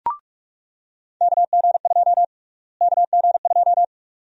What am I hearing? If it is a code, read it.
Morse code, heard as a keyed tone, about 46 wpm, 700 Hz. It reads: KG2 KG2
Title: USA Call Sign Prefixes - Repeated 2x - 45wpm